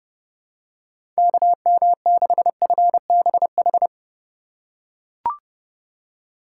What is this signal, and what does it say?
Morse code, heard as a keyed tone, about 30 wpm, 700 Hz. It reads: KM6FBH
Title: USA Call Signs - 6 Characters - 3x Character Spacing - 30wpm